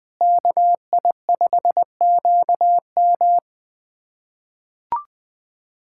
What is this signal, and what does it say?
Morse code, heard as a keyed tone, about 20 wpm, 700 Hz. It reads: KI5QM
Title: USA Callsign DE Callsign 20wpm